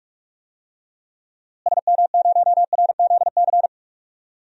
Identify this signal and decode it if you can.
Morse code, heard as a keyed tone, about 44 wpm, 700 Hz. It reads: SM0RZC
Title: DXCC-100 Call Signs - 6 Characters - Repeated 2x - 45wpm